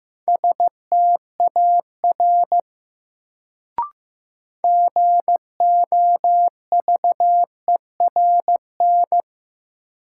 Morse code, heard as a keyed tone, about 15 wpm, 700 Hz. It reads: STAR GOVERN
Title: Sentences from Top 1000 Words - Review Part 2 - New Word Drill - 15wpm